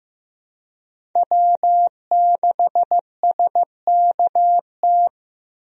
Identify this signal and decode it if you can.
Morse code, heard as a keyed tone, about 15 wpm, 700 Hz. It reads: W6SKT
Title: USA Call Signs - 5 Characters - Repeated 2x - 15wpm